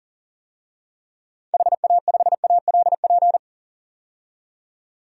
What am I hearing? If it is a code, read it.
Morse code, heard as a keyed tone, about 40 wpm, 700 Hz. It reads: HA5ALP
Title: DXCC-100 Call Signs - 6 Characters - 40wpm